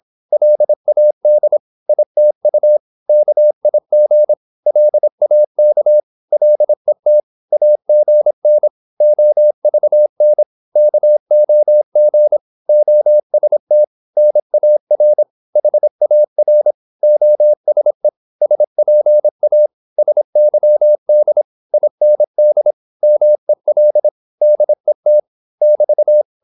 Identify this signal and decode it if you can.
Morse code, heard as a keyed tone, about 26 wpm, 600 Hz. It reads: LAD ITU KIG LAK LET AGN OVN KOG OST NAR HAR OSE SPA SYD IND MEL DET <BT>
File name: Korte ord DK 26wpm